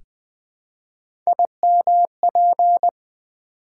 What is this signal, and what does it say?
Morse code, heard as a keyed tone, about 20 wpm, 700 Hz. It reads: IMP
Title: Build A Word - 1 Letter At A Time - 5-Letter Words - 20wpm